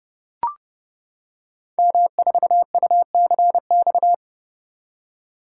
Morse code, heard as a keyed tone, about 28 wpm, 700 Hz. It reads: M4UCX
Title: Code Groups - 5 Letter:Numbers - 30wpm